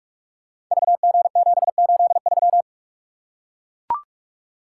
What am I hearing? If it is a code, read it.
Morse code, heard as a keyed tone, about 44 wpm, 700 Hz. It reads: VG783
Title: Code Groups - 5 Letter:Numbers - 45wpm